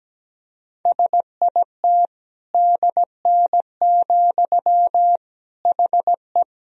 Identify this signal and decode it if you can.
Morse code, heard as a keyed tone, about 17 wpm, 700 Hz. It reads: SIT DN, HE
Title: Sentences from Top 500 Words and Abbreviations 17wpm